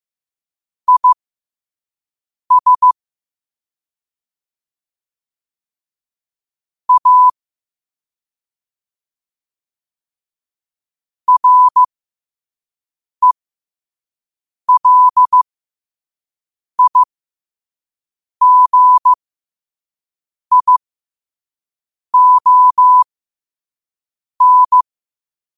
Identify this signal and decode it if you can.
Morse code, heard as a keyed tone, about 15 wpm, spaced as for 5 wpm, 1000 Hz. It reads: IS A RELIGION
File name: qotdfiles / 5wpm